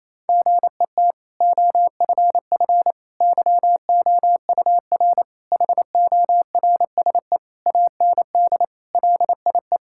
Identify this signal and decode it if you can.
Morse code, heard as a keyed tone, about 28 wpm, 700 Hz. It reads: GET OFF YOUR HORSE AND LIE
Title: Sentences from Top 1600 Words 28wpm